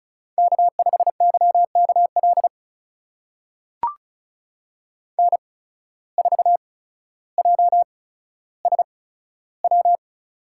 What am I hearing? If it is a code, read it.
Morse code, heard as a keyed tone, about 36 wpm, spaced as for 15 wpm, 700 Hz. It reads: K5YKL N4JSW
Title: USA Call Signs - 5 Characters - 8x Character Spacing - 35wpm